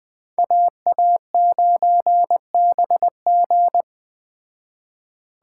Morse code, heard as a keyed tone, about 20 wpm, 700 Hz. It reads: AA9BG
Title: Speed Racing - USA 2x2 Call Signs 17wpm